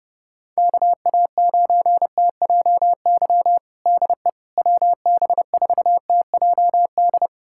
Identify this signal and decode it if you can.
Morse code, heard as a keyed tone, about 30 wpm, 700 Hz. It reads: KA9TJY DE WB4TJD
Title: USA Callsign DE Callsign 30wpm